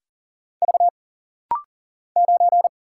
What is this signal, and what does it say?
Morse code, heard as a keyed tone, about 40 wpm, 700 Hz. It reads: V 9
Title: Single Letter-Number - Rapid-Fire 40wpm